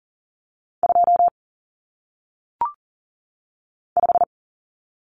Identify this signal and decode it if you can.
Morse code, heard as a keyed tone, about 40 wpm, 700 Hz. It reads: 2 5
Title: Common Cut Numbers - 1-Digit 40wpm